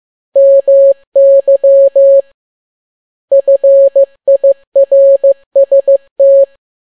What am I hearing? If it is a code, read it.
Morse code, heard as a keyed tone, about 15 wpm, 550 Hz. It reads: MY FIRST
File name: n6tt p1 15C13